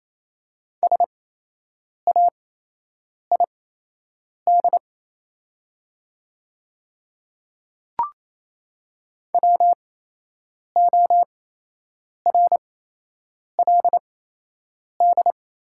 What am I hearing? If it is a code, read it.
Morse code, heard as a keyed tone, about 28 wpm, spaced as for 8 wpm, 700 Hz. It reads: SAID WORLD
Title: 8x Character Spacing - Top 200 Words - 28wpm